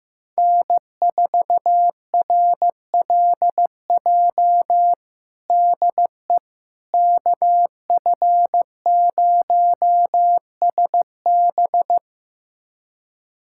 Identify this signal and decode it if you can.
Morse code, heard as a keyed tone, about 15 wpm, 700 Hz. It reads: N4RLJ DE KF0SB
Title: USA Callsign DE Callsign - 5 Characters - 15wpm